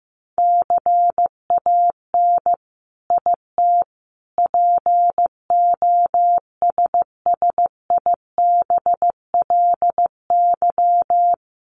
Morse code, heard as a keyed tone, about 15 wpm, 700 Hz. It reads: CAN IT POSSIBLY
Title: Sentences from Top 1300 Words 15wpm